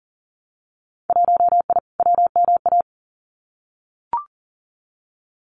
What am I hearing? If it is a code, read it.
Morse code, heard as a keyed tone, about 38 wpm, 700 Hz. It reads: 1I WMA
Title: Winter Field Day Exchanges Encore 40wpm